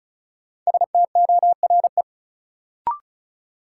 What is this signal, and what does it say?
Morse code, heard as a keyed tone, about 36 wpm, 700 Hz. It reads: STORE
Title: Sentences from Top 1000 Words - Review Part 5 - New Word Drill - 35wpm